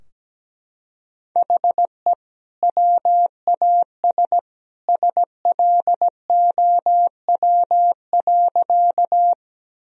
Morse code, heard as a keyed tone, about 17 wpm, 700 Hz. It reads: HE WAS SLOW.
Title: Sentences from Top 1000 Words - Review Part 2 - 17wpm